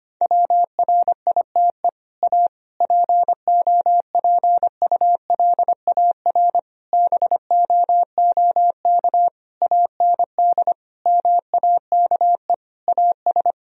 Morse code, heard as a keyed tone, about 25 wpm, 700 Hz. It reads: WRITE A POPULAR BOOK AND MAKE AS
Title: Sentences from Top 1000 Words - Review Part 3 - 25wpm